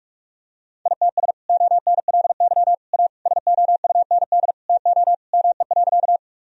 Morse code, heard as a keyed tone, about 46 wpm, 700 Hz. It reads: ITS ONLY A SOUND TO ME.
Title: Sentences from Top 1000 Words - Review Part 1 - 45wpm